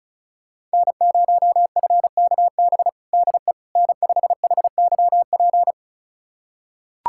Morse code, heard as a keyed tone, about 36 wpm, 700 Hz. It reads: N0FKB DE N5HYP
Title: USA Callsign DE Callsign - 5 Characters - 35wpm